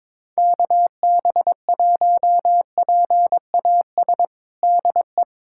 Morse code, heard as a keyed tone, about 22 wpm, 700 Hz. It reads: KB1PAS DE
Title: USA Callsign DE Callsign 22wpm